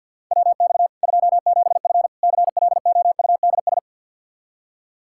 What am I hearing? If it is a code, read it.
Morse code, heard as a keyed tone, about 52 wpm, 700 Hz. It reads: WX 27F CLOUDS
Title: Practice QSO - Exchange 2 - Info - 50wpm